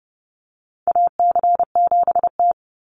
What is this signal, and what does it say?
Morse code, heard as a keyed tone, about 30 wpm, 700 Hz. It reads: AC7T
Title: Call Signs - 30wpm